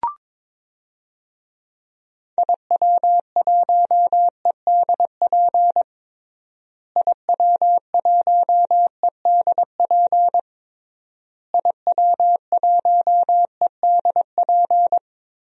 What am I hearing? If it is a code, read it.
Morse code, heard as a keyed tone, about 22 wpm, 700 Hz. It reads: IW1EDP IW1EDP IW1EDP
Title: DXCC-100 Call Signs - 6 Characters - Repeated 3x - 22wpm